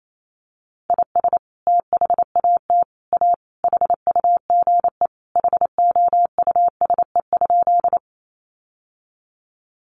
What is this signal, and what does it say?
Morse code, heard as a keyed tone, about 28 wpm, 700 Hz. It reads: IS THAT A HUGE HOUSE?
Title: Sentences from Top 1100 Words 28wpm